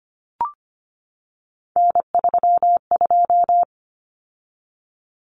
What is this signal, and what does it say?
Morse code, heard as a keyed tone, about 25 wpm, 700 Hz. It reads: N32
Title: Common Cut Numbers - 3-Digit 25wpm